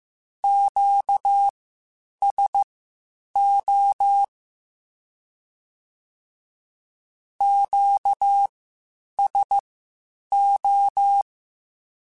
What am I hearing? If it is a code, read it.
Morse code, heard as a keyed tone, about 15 wpm, spaced as for 8 wpm, 800 Hz. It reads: QSO QSO
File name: (spaced) / Q-Calls Spaced-mp3s / Q Calls Twice sp @13